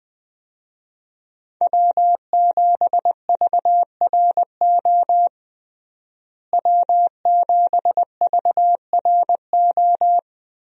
Morse code, heard as a keyed tone, about 20 wpm, 700 Hz. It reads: W7VRO W7VRO
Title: USA Call Signs - 5 Characters - Repeated 3x - 20wpm